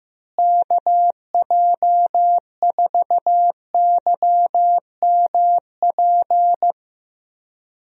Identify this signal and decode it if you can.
Morse code, heard as a keyed tone, about 15 wpm, 700 Hz. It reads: KJ4YMP
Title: USA Call Signs - 6 Characters - Repeated 3x - 15wpm